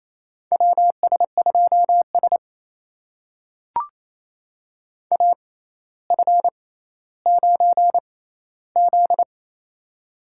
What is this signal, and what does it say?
Morse code, heard as a keyed tone, about 28 wpm, spaced as for 14 wpm, 700 Hz. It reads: WS2S AF9Z
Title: USA Call Signs - 4 Characters - 6x Character Spacing - 28wpm